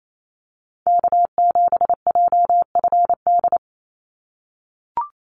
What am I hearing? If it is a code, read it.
Morse code, heard as a keyed tone, about 28 wpm, 700 Hz. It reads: K7JFD
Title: NA SOTA Activator Callsigns 28wpm